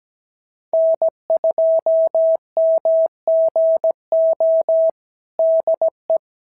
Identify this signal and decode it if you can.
Morse code, heard as a keyed tone, about 16 wpm, 650 Hz. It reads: N2MGO DE
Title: SOTA QSO Practice 17wpm